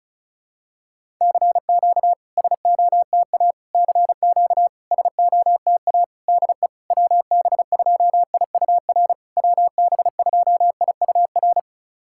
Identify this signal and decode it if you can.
Morse code, heard as a keyed tone, about 34 wpm, 700 Hz. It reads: CQ SOTA CQ SOTA DE WB2IUR WB2IUR
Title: SOTA QSO Practice 35wpm